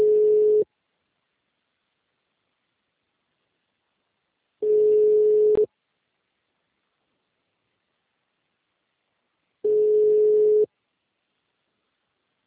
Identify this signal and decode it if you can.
Ringback tone (Europe, 425 Hz).
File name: out- -2030- - - .26901